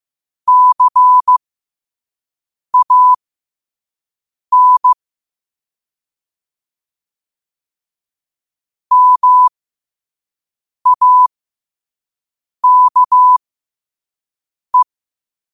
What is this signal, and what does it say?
Morse code, heard as a keyed tone, about 15 wpm, spaced as for 5 wpm, 1000 Hz. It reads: CAN MAKE
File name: qotdfiles / 5wpm